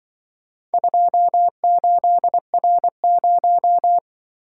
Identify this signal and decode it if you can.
Morse code, heard as a keyed tone, about 25 wpm, 700 Hz. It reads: 28R0
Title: Speed-Racing - Four Letter-Number Combinations - 20wpm